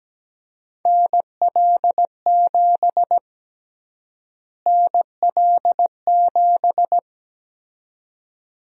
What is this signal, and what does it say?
Morse code, heard as a keyed tone, about 17 wpm, 700 Hz. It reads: NL7 NL7
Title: USA Call Sign Prefixes - Repeated 2x - 17wpm